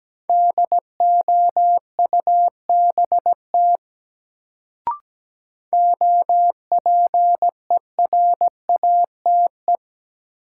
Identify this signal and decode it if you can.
Morse code, heard as a keyed tone, about 17 wpm, 700 Hz. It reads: DOUBT OPERATE
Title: Sentences from Top 1500 Words - New Word Drill - 17wpm